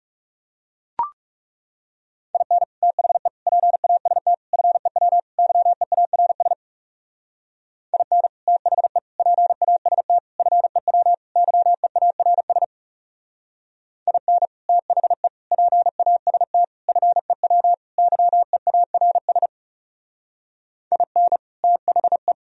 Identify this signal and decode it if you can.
Morse code, heard as a keyed tone, about 36 wpm, 700 Hz. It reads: IN THE PAST FEW YEARS IN THE PAST FEW YEARS IN THE PAST FEW YEARS IN THE
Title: Speed-Racing - Sets of 5 Words 30wpm